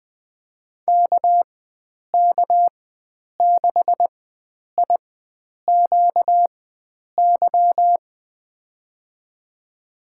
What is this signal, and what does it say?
Morse code, heard as a keyed tone, about 20 wpm, spaced as for 9 wpm, 700 Hz. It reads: KK6IQY
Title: USA Call Signs - 6 Characters - 4x Character Spacing - 20wpm